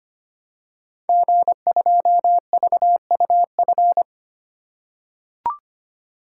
Morse code, heard as a keyed tone, about 25 wpm, 700 Hz. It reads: G2VUF
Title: Code Groups - 5 Letter:Numbers - 25wpm